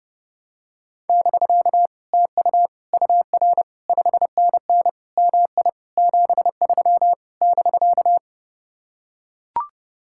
Morse code, heard as a keyed tone, about 30 wpm, 700 Hz. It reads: <BK> TU UR 5NN MI 73 <BK>